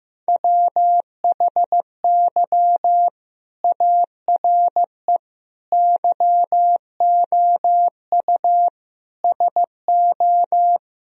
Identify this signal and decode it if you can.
Morse code, heard as a keyed tone, about 15 wpm, 700 Hz. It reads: WHY ARE YOU SO
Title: Sentences from Top 1000 Words - Review Part 3 - 15wpm